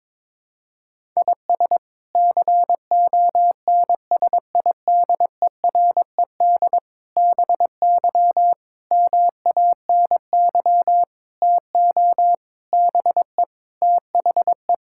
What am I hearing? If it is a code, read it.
Morse code, heard as a keyed tone, about 21 wpm, 700 Hz. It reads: IS CONSIDERED BY MANY TO BE THE
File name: practice / 22wpm / Sets of 7 Words - 22wpm